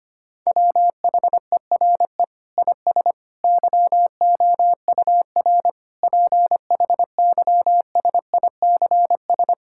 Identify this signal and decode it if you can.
Morse code, heard as a keyed tone, about 25 wpm, 700 Hz. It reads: WHERE IS YOUR PHYSICS
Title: Sentences from Top 1000 Words - Review Part 5 - 25wpm